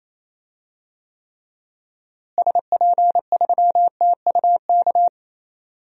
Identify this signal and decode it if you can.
Morse code, heard as a keyed tone, about 28 wpm, 700 Hz. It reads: SP3TUK
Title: DXCC-100 Call Signs - 6 Characters - Repeated 3x - 28wpm